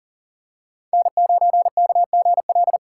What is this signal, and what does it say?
Morse code, heard as a keyed tone, about 40 wpm, 700 Hz. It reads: N9KGL
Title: USA Call Signs - 5 Characters - 4x Character Spacing - 40wpm